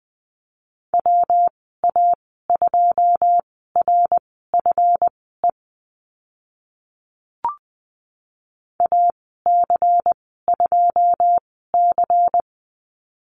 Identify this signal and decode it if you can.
Morse code, heard as a keyed tone, about 20 wpm, spaced as for 14 wpm, 700 Hz. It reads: WA2RFE AC2C
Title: 140 Lesson 23g - Letter f - Call Signs 20wpm